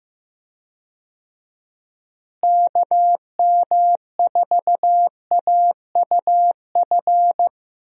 Morse code, heard as a keyed tone, about 14 wpm, 700 Hz. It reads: KM4AUF
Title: USA Call Signs - 6 Characters - Repeated 3x - 15wpm